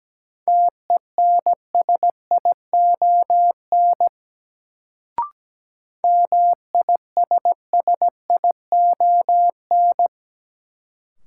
Morse code, heard as a keyed tone, about 17 wpm, 700 Hz. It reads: TENSION MISSION
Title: Sentences from Top 1600 Words - New Word Drill Alt - 17wpm